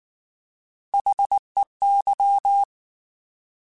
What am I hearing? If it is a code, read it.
Morse code, heard as a keyed tone, about 19 wpm, 800 Hz. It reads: HEY